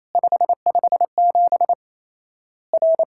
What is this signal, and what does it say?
Morse code, heard as a keyed tone, about 28 wpm, 700 Hz. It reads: 557 R
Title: Worked All Europe DX Contest - QTC Exchange - 28wpm